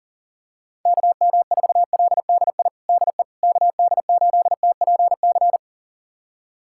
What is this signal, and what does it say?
Morse code, heard as a keyed tone, about 40 wpm, 700 Hz. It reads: KM4LDI DE KD8TPC
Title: USA Callsign DE Callsign - 6 Characters - 40wpm